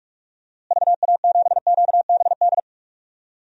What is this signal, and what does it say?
Morse code, heard as a keyed tone, about 42 wpm, 700 Hz. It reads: VA7QBD
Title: DXCC-100 Call Signs - 6 Characters - Repeated 3x - 45wpm